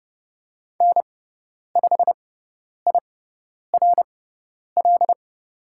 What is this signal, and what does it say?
Morse code, heard as a keyed tone, about 30 wpm, spaced as for 10 wpm, 700 Hz. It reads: N5IRL
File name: Lesson 17g - Letter u - Call Signs 30wpm